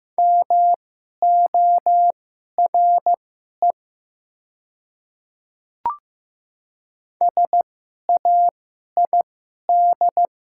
Morse code, heard as a keyed tone, about 15 wpm, spaced as for 11 wpm, 700 Hz. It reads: MORE SAID
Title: 2x Character Spacing - Top 200 Words - 15wpm